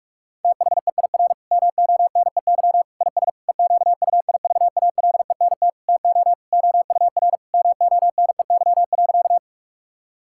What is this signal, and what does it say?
Morse code, heard as a keyed tone, about 46 wpm, 700 Hz. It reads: THEIR MONEY IS EQUIVALENT TO OUR MONEY.
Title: Sentences from Top 1600 Words 45wpm